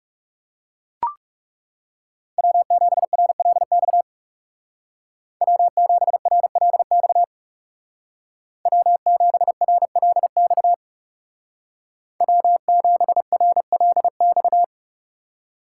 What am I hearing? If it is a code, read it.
Morse code, heard as a keyed tone, about 38 wpm, 700 Hz. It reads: W7RLX W7RLX W7RLX W7RLX